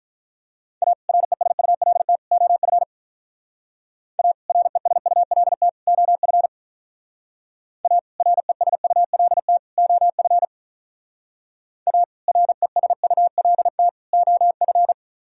Morse code, heard as a keyed tone, about 46 wpm, 700 Hz. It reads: A RESULT OF A RESULT OF A RESULT OF A RESULT OF